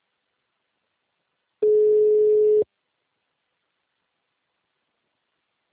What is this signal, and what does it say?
Ringback tone (Europe, 425 Hz).
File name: out- -2005- - - .43449